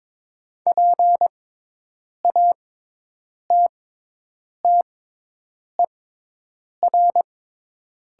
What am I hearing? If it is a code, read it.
Morse code, heard as a keyed tone, about 22 wpm, spaced as for 8 wpm, 700 Hz. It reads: PATTER
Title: 6x Character Spacing - Top 500 Words - 22wpm